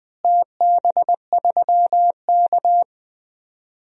Morse code, heard as a keyed tone, about 21 wpm, 700 Hz. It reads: TB3K
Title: DXCC-100 Call Signs - 4 Characters - Repeated 2x - 20wpm